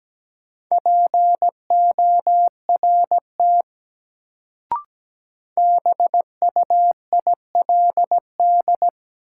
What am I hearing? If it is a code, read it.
Morse code, heard as a keyed tone, about 17 wpm, 700 Hz. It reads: PORT BUILD
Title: Sentences from Top 1000 Words - Review Part 1 - New Word Drill - 17wpm